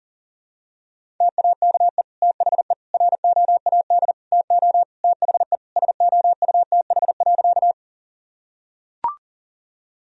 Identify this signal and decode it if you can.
Morse code, heard as a keyed tone, about 37 wpm, 700 Hz. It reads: TAKE THE ROAD TO THE SOUTH.
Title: Sentences from Top 1000 Words - Review Part 2 - 40wpm